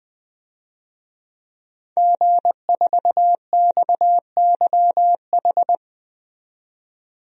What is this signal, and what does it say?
Morse code, heard as a keyed tone, about 20 wpm, 700 Hz. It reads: G4XYH
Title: DXCC-100 Call Signs - 5 Characters - 20wpm